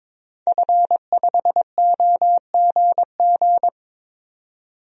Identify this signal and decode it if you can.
Morse code, heard as a keyed tone, about 22 wpm, 700 Hz. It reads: F5OGG
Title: DXCC-100 Call Signs - 5 Characters - 22wpm